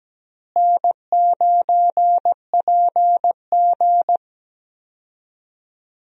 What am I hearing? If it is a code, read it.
Morse code, heard as a keyed tone, about 17 wpm, 700 Hz. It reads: N9PG
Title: USA Call Signs - 4 Characters - Repeated 2x - 17wpm